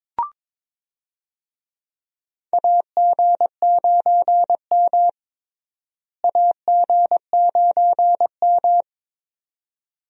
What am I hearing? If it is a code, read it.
Morse code, heard as a keyed tone, about 22 wpm, 700 Hz. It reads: AG9M AG9M
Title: USA Call Signs - 4 Characters - Repeated 2x - 22wpm